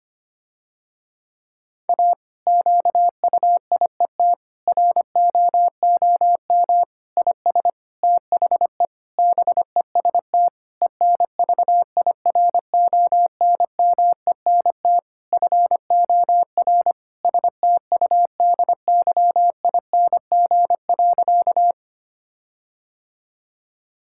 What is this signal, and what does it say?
Morse code, heard as a keyed tone, about 25 wpm, 700 Hz. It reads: A QUIET ROOM IS THE BEST ENVIRONMENT FOR STUDYING.
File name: practice / 25wpm / Difficult Sentences with Hints - 25wpm